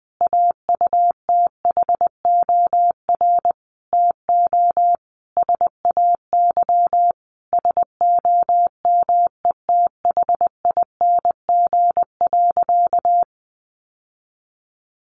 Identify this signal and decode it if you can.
Morse code, heard as a keyed tone, about 20 wpm, 700 Hz. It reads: AUTHOR TO SAY SOMETHING.
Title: Sentences from Top 1200 Words 20wpm